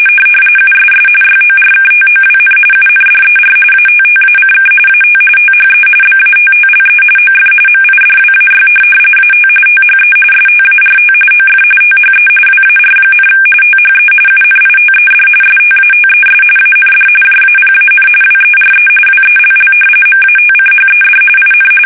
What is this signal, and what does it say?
RTTY, about 45.45 baud. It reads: USS YORKTOWN UNDER ATTACK AT THE BATTLE OF MIDWAY AT THE START OF THE WAR, THE UNITED STATES AND JAPAN WERE WELL MATCHED IN
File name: fsk850